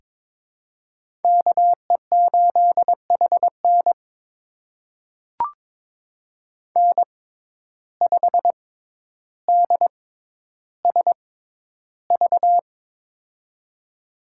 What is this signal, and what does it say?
Morse code, heard as a keyed tone, about 22 wpm, spaced as for 11 wpm, 700 Hz. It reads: KE8HN N5DSV
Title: USA Call Signs - 5 Characters - 6x Character Spacing - 22wpm